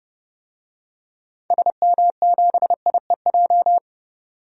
Morse code, heard as a keyed tone, about 30 wpm, 700 Hz. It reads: SM7IEJ
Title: DXCC-100 Call Signs - 6 Characters - 30wpm